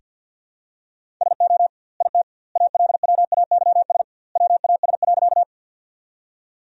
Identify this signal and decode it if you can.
Morse code, heard as a keyed tone, about 49 wpm, 700 Hz. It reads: SO IT ALWAYS WAS.
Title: 2x Word-Spacing - Sentences from Top 300 Words - 50wpm